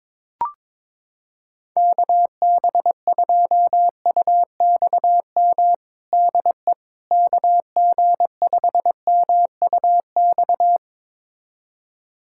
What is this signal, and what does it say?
Morse code, heard as a keyed tone, about 22 wpm, 700 Hz. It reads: KB2UXM DE KG5MUX
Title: USA Callsign DE Callsign - 6 Characters - 22wpm